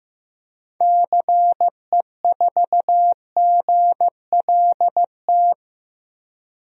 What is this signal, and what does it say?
Morse code, heard as a keyed tone, about 15 wpm, 700 Hz. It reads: CE4GLT
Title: DXCC-100 Call Signs - 6 Characters - 15wpm